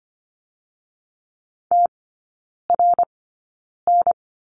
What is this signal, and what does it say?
Morse code, heard as a keyed tone, about 25 wpm, spaced as for 9 wpm, 700 Hz. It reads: TRN
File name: Lesson 17e - Letter u - Three Characters 25wpm